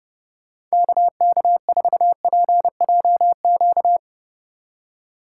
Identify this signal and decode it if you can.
Morse code, heard as a keyed tone, about 36 wpm, 700 Hz. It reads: KK4PJQ
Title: Speed Racing - USA 2x3 Call Signs 30wpm